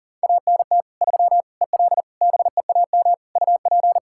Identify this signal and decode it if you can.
Morse code, heard as a keyed tone, about 40 wpm, 700 Hz. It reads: ANT 3 EL BEAM UP
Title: Practice QSO - Exchange 2 - Info - 40wpm